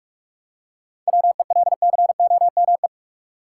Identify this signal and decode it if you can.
Morse code, heard as a keyed tone, about 44 wpm, 700 Hz. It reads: WELCOME
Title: Seven Letter Words - 45wpm